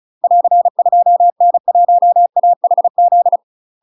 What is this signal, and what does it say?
Morse code, heard as a keyed tone, about 34 wpm, 700 Hz. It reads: <AR> 2N1AHZ
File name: UK MASTER QSOs TRAINING / 35WPM / UK MASTER QSOs 0024